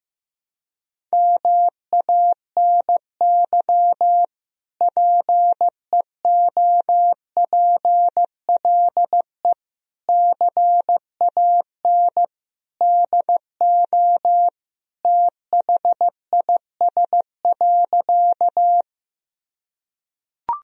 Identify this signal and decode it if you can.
Morse code, heard as a keyed tone, about 16 wpm, 700 Hz. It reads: MANY PEOPLE CAN DO THIS.
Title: Sentences from Top 1000 Words - Review Part 1 - 15wpm